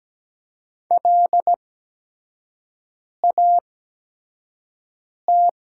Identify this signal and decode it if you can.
Morse code, heard as a keyed tone, about 17 wpm, spaced as for 5 wpm, 700 Hz. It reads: LAT
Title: 8x Character Spacing - Top 300 Words - 17wpm